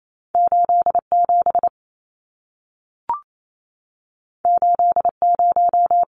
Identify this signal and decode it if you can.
Morse code, heard as a keyed tone, about 28 wpm, 700 Hz. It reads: 87 80
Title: Numbers - 2-Digit 28wpm